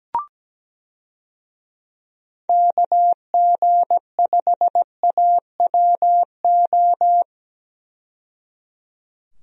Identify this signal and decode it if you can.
Morse code, heard as a keyed tone, about 17 wpm, 700 Hz. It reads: KG5AWO
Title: USA Call Signs - 6 Characters - 17wpm